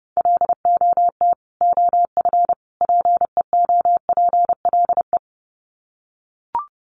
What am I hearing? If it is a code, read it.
Morse code, heard as a keyed tone, about 30 wpm, 700 Hz. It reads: LOT OF PEOPLE
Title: Sets of 3 Words 30wpm